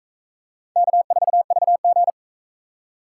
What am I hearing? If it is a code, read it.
Morse code, heard as a keyed tone, about 42 wpm, 700 Hz. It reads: K4VG